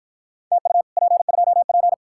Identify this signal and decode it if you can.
Morse code, heard as a keyed tone, about 53 wpm, 700 Hz. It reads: TU P2P